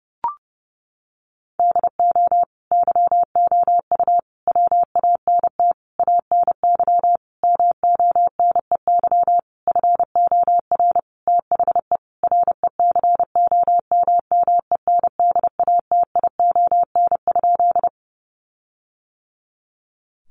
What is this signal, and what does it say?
Morse code, heard as a keyed tone, about 30 wpm, 700 Hz. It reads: DO YOU WANT ANY MONEY FOR THE RECOMMENDATION?
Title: Sentences from Top 1400 Words 30wpm